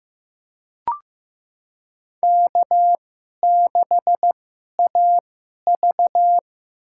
Morse code, heard as a keyed tone, about 15 wpm, spaced as for 11 wpm, 700 Hz. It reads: K6AV
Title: USA Call Signs - 2x Character Spacing - 15wpm